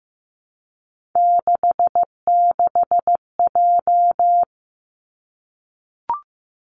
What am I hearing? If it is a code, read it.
Morse code, heard as a keyed tone, about 15 wpm, 700 Hz. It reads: 66J
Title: Three Challenging Letter-Number Combinations 15wpm